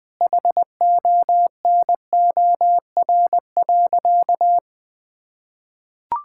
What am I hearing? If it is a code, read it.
Morse code, heard as a keyed tone, about 19 wpm, 700 Hz. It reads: HONOR.
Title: Sentences with 6-Word Phrases - 20wpm